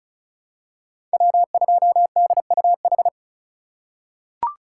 Morse code, heard as a keyed tone, about 36 wpm, 700 Hz. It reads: W2DUH
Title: USA Call Signs - 5 Characters - 2x Character Spacing - 35wpm